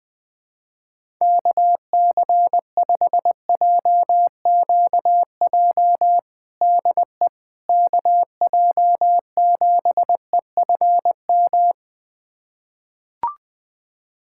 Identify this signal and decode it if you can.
Morse code, heard as a keyed tone, about 20 wpm, 700 Hz. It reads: KC5JQJ DE KJ7EFM
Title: USA Callsign DE Callsign - 6 Characters - 20wpm